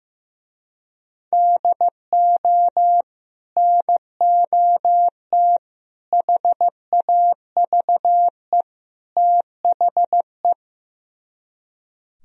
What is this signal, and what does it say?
Morse code, heard as a keyed tone, about 15 wpm, 700 Hz. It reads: DO NOT HAVE THE
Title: Speed-Racing - Sets of 4 Words 15wpm